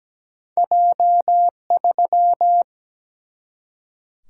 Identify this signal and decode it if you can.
Morse code, heard as a keyed tone, about 17 wpm, 700 Hz. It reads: J3
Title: Single-Letter Single-Number 17wpm